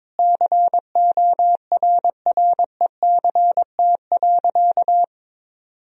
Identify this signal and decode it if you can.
Morse code, heard as a keyed tone, about 22 wpm, 700 Hz. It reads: CORRECT.
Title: Sentences from Top 1000 Words - Review Part 3 - 22wpm